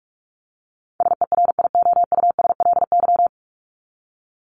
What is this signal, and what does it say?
Morse code, heard as a keyed tone, about 44 wpm, 700 Hz. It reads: SERIOUSLY
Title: LY Suffix - Top 100 Words with Base Words 45wpm